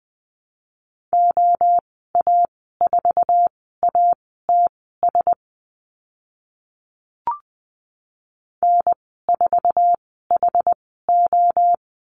Morse code, heard as a keyed tone, about 20 wpm, spaced as for 14 wpm, 700 Hz. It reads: OA4ATS N4HO
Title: 069 Lesson 12g - Letter d - Call Signs 20wpm